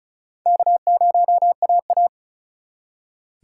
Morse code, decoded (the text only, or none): K0AA